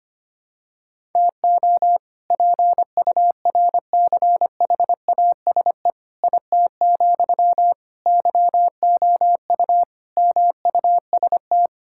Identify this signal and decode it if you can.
Morse code, heard as a keyed tone, about 25 wpm, 700 Hz. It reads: TO PURCHASE IT, YOU MUST
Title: Sentences from Top 1000 Words - Review Part 5 - 25wpm